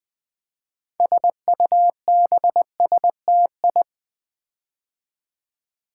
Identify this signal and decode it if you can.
Morse code, heard as a keyed tone, about 20 wpm, 700 Hz. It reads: SUBSTI
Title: Build A Word - 6 Letters At A Time - 12-Letter Words - 20wpm